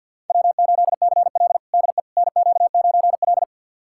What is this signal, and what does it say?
Morse code, heard as a keyed tone, about 52 wpm, 700 Hz. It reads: W8CL DE NQ9L
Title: USA Callsign DE Callsign - 4 Characters - 50wpm